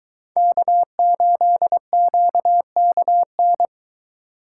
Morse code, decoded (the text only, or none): K8QKN